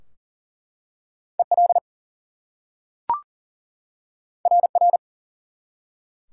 Morse code, heard as a keyed tone, about 38 wpm, 700 Hz. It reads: EL RR
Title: Top 180 QSO Elements 40wpm